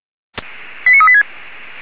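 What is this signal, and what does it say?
Five-tone selective call (ZVEI), heard as quick runs of five tones, tones of about 70 ms. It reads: 89386
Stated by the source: ZVEI five-tone selective call received on an SDR.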